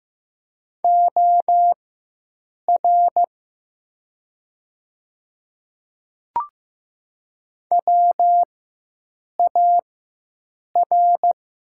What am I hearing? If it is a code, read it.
Morse code, heard as a keyed tone, about 15 wpm, spaced as for 7 wpm, 700 Hz. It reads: OR WAR
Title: 4x Character Spacing - Top 500 Words - 15wpm